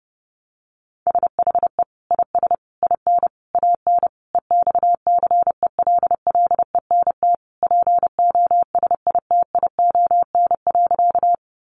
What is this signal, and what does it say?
Morse code, heard as a keyed tone, about 28 wpm, 700 Hz. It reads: SHE IS IN AN EXCELLENT POSITION.
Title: Sentences from Top 1000 Words 30wpm